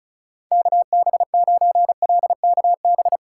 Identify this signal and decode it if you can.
Morse code, heard as a keyed tone, about 34 wpm, 700 Hz. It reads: KB9LKB
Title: USA Call Signs - 6x Character Spacing - 35wpm